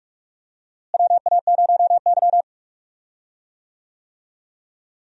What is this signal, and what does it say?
Morse code, heard as a keyed tone, about 44 wpm, 700 Hz. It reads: WA0Y
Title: USA Call Signs - 4 Characters - Repeated 3x - 45wpm